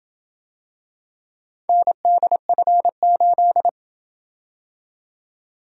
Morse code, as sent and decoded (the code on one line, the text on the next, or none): -. -.. ..-. ---..
NDF8